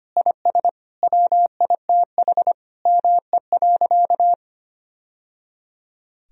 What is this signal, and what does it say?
Morse code, heard as a keyed tone, about 25 wpm, 700 Hz. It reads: IS WITH ME.
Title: Sentences from Top 1000 Words - Review Part 3 - 25wpm